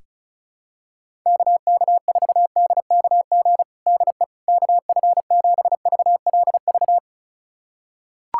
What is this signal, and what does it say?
Morse code, heard as a keyed tone, about 36 wpm, 700 Hz. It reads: KK4DKG DE KF7VLV
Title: USA Callsign DE Callsign 35wpm